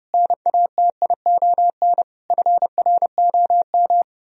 Morse code, heard as a keyed tone, about 40 wpm, spaced as for 35 wpm, 700 Hz. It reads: NATION FROM
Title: Speed-Racing - Sets of 5 Words 30wpm